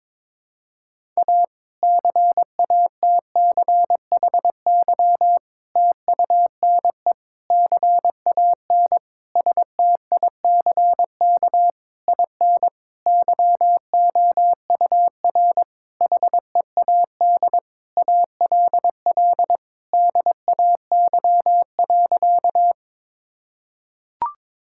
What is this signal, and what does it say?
Morse code, heard as a keyed tone, about 22 wpm, 700 Hz. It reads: A CATCHY TUNE CAN STICK IN YOUR HEAD ALL DAY.
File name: Difficult Sentences with Hints - 22wpm